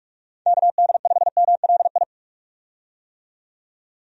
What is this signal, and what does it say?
Morse code, heard as a keyed tone, about 47 wpm, 700 Hz. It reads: KD5MLI
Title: USA Call Signs - 6 Characters - 45wpm